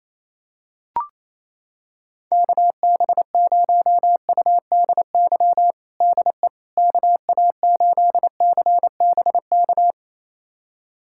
Morse code, heard as a keyed tone, about 28 wpm, 700 Hz. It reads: KB0UDY DE KA8CBK
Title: USA Callsign DE Callsign - 6 Characters - 28wpm